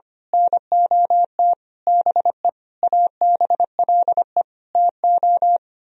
Morse code, heard as a keyed tone, about 24 wpm, 700 Hz. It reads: NOT BE ABLE TO
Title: Speed-Racing - Sets of 5 Words 25wpm